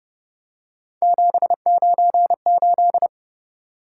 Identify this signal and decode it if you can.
Morse code, heard as a keyed tone, about 31 wpm, 700 Hz. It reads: 798